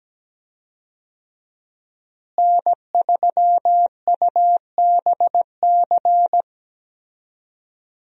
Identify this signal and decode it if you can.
Morse code, heard as a keyed tone, about 17 wpm, 700 Hz. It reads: N3UBC
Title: USA Call Signs - 5 Characters - Repeated 3x - 17wpm